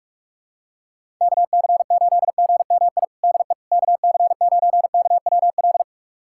Morse code, heard as a keyed tone, about 44 wpm, 700 Hz. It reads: KC8GMI DE KC9KWL
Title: USA Callsign DE Callsign - 6 Characters - 45wpm